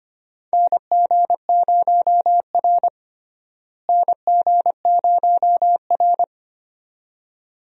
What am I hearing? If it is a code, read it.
Morse code, heard as a keyed tone, about 25 wpm, 700 Hz. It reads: NG0R NG0R
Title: USA Call Signs - 4 Characters - Repeated 2x - 25wpm